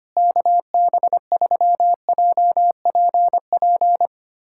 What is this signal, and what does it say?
Morse code, heard as a keyed tone, about 25 wpm, 700 Hz. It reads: KB3JPP
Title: USA Callsign DE Callsign - 6 Characters - 25wpm